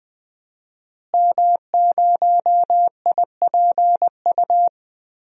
Morse code, heard as a keyed tone, about 20 wpm, 700 Hz. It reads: M0IPU
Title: DXCC-100 Call Signs - 5 Characters - Repeated 2x - 20wpm